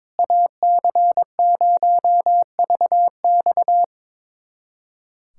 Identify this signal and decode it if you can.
Morse code, heard as a keyed tone, about 22 wpm, 700 Hz. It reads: AC0VX
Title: USA Callsign DE Callsign - 5 Characters - 22wpm